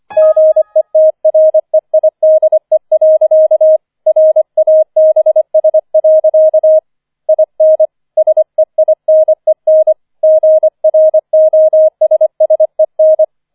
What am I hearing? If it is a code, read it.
Morse code, heard as a keyed tone, about 25 wpm, 630 Hz. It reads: GETREIDE. RABS. IN SEINEN GROSSEN